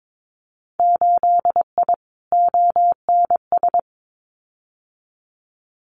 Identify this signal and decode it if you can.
Morse code, heard as a keyed tone, about 22 wpm, 700 Hz. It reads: 8I ONS
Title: Winter Field Day Exchanges Encore 22wpm